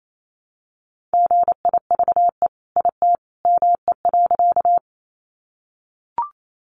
Morse code, heard as a keyed tone, about 28 wpm, 700 Hz. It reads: GIVE IT ME.
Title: Sentences from Top 200 Words 28wpm